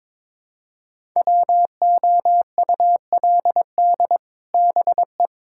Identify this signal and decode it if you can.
Morse code, heard as a keyed tone, about 22 wpm, 700 Hz. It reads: WOULD BE
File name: Build A Phrase - 2-Word Phrases - 22wpm